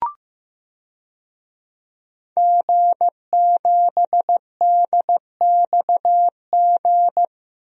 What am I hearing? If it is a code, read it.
Morse code, heard as a keyed tone, about 15 wpm, 700 Hz. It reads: G7DXG
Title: DXCC-100 Call Signs - 5 Characters - Repeated 3x - 15wpm